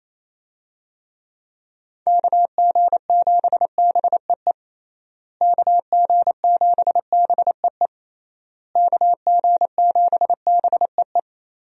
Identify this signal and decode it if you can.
Morse code, heard as a keyed tone, about 28 wpm, 700 Hz. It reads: KG7BEE KG7BEE KG7BEE
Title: USA Call Signs - 6 Characters - Repeated 3x - 28wpm